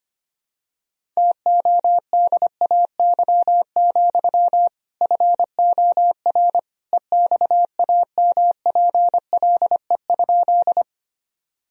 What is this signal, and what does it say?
Morse code, heard as a keyed tone, about 25 wpm, 700 Hz. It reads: TODAY, FOR EXAMPLE?
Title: Sentences from Top 1000 Words - Review Part 1 - 25wpm